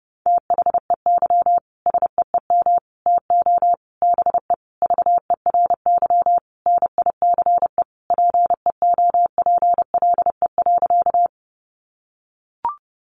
Morse code, heard as a keyed tone, about 30 wpm, 700 Hz. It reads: THEY SEEM TO BE VERY NICE PEOPLE.
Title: Sentences from Top 1100 Words 30wpm